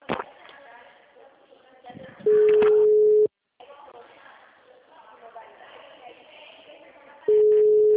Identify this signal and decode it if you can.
Ringback tone (Europe, 425 Hz), with speech.